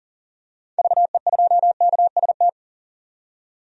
Morse code, heard as a keyed tone, about 40 wpm, 700 Hz. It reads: VE2KST
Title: DXCC-100 Call Signs - 6 Characters - 40wpm